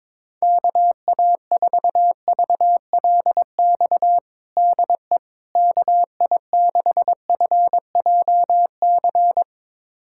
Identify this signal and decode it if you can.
Morse code, heard as a keyed tone, about 22 wpm, 700 Hz. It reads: KA4VLX DE KI6FJC
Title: USA Callsign DE Callsign - 6 Characters - 22wpm